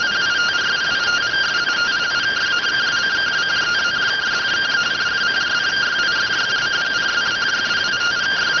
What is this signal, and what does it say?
Two-tone teleprinter signal, about 170 Hz shift.